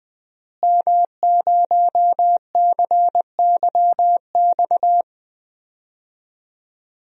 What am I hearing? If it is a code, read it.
Morse code, heard as a keyed tone, about 20 wpm, 700 Hz. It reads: M0CYX
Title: DXCC-100 Call Signs - 5 Characters - Repeated 3x - 20wpm